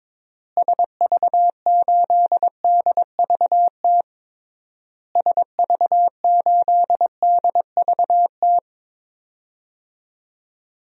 Morse code, heard as a keyed tone, about 22 wpm, 700 Hz. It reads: SV8DVT SV8DVT
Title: DXCC-100 Call Signs - 6 Characters - Repeated 3x - 22wpm